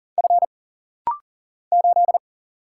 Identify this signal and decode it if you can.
Morse code, heard as a keyed tone, about 40 wpm, 700 Hz. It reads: F 8
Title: Single Letter-Number - Rapid-Fire 40wpm